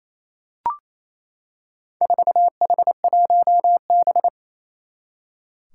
Morse code, heard as a keyed tone, about 28 wpm, 700 Hz. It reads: 4H1B
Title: Four Challenging Letter-Number Combinations 28wpm